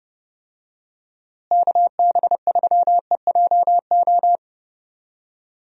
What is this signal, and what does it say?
Morse code, heard as a keyed tone, about 30 wpm, 700 Hz. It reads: KB3EJO
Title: USA Call Signs - 6 Characters - Repeated 3x - 30wpm